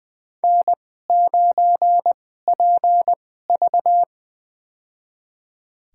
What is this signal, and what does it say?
Morse code, heard as a keyed tone, about 20 wpm, spaced as for 14 wpm, 700 Hz. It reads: N9PV
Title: USA Call Signs - 2x Character Spacing - 20wpm